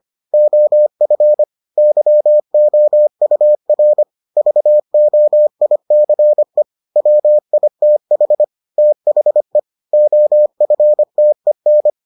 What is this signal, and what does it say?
Morse code, heard as a keyed tone, about 25 wpm, 600 Hz. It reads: OF YOUR VOICE WITH THE OFTEN